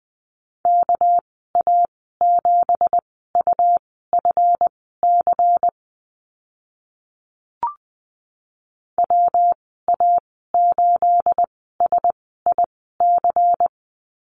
Morse code, heard as a keyed tone, about 20 wpm, spaced as for 14 wpm, 700 Hz. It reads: KA7UFC WA8SIC
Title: 231 Lesson 37g - Letter x - Call Signs 20wpm